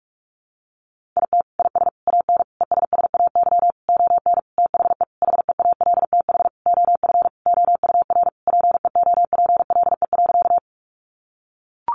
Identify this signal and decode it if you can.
Morse code, heard as a keyed tone, about 44 wpm, 700 Hz. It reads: IT IS AN ESSAY ON THE HEALTH OF OUR PEOPLE.
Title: Sentences from Top 1500 Words 45wpm